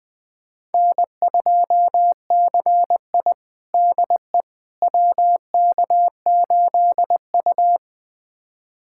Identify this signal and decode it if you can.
Morse code, heard as a keyed tone, about 21 wpm, 700 Hz. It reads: N2CI DE WK8U
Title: USA Callsign DE Callsign - 4 Characters - 20wpm